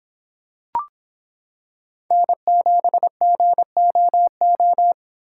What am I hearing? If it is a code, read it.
Morse code, heard as a keyed tone, about 26 wpm, 700 Hz. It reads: N7GOO